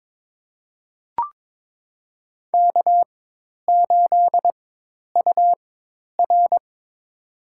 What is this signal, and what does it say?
Morse code, heard as a keyed tone, about 22 wpm, spaced as for 10 wpm, 700 Hz. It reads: K8UR
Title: USA Call Signs - 4 Characters - 4x Character Spacing - 22wpm